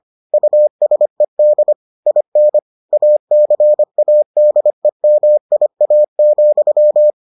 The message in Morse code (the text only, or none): USED IN ACADEMIA,